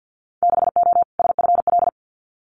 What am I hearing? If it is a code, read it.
Morse code, heard as a keyed tone, about 50 wpm, 700 Hz. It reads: 6O SFL
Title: Winter Field Day Exchanges Encore 50wpm